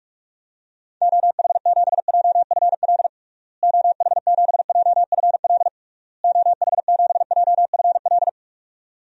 Morse code, heard as a keyed tone, about 46 wpm, 700 Hz. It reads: OH7JFL OH7JFL OH7JFL
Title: DXCC-100 Call Signs - 6 Characters - Repeated 3x - 45wpm